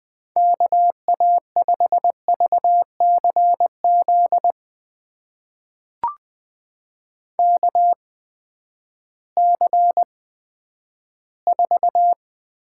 Morse code, heard as a keyed tone, about 20 wpm, spaced as for 11 wpm, 700 Hz. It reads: KA5VCZ KC4
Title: USA Call Signs - 6 Characters - 8x Character Spacing - 20wpm